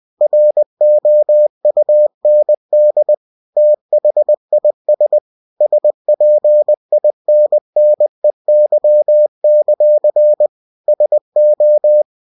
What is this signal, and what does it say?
Morse code, heard as a keyed tone, about 20 wpm, 600 Hz. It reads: ROUND THIS SPINNEY; SO
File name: cw-books / winnie-the-pooh / winnie-20wpm-page-0022